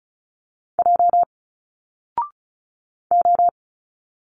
Morse code, heard as a keyed tone, about 34 wpm, 700 Hz. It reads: J O
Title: Single Letters 35wpm